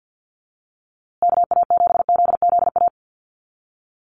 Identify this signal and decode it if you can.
Morse code, heard as a keyed tone, about 50 wpm, 700 Hz. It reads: KA7ZZA
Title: USA Call Signs Repeated 3x 50wpm